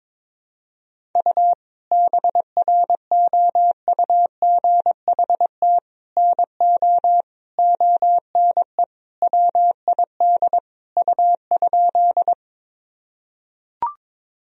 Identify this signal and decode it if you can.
Morse code, heard as a keyed tone, about 22 wpm, 700 Hz. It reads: U BROUGHT NO ONE WID U?
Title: Sentences from Top 500 Words and Abbreviations 22wpm